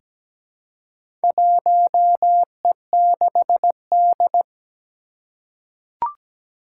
Morse code, heard as a keyed tone, about 17 wpm, 700 Hz. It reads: 1E6D
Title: Four Challenging Letter-Number Combinations 17wpm